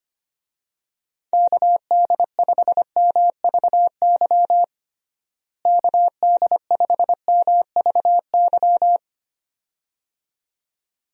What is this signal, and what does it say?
Morse code, heard as a keyed tone, about 25 wpm, 700 Hz. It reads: KD5MVY KD5MVY
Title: USA Call Signs - 6 Characters - Repeated 2x - 25wpm